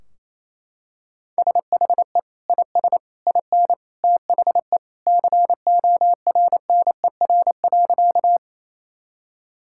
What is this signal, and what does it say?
Morse code, heard as a keyed tone, about 28 wpm, 700 Hz. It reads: SHE IS IN THE CORNER.
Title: Sentences from Top 1600 Words 28wpm